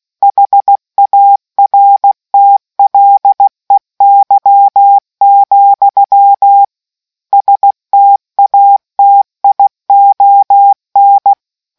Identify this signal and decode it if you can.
Morse code, heard as a keyed tone, about 16 wpm, 800 Hz. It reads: HARTLEY, STATION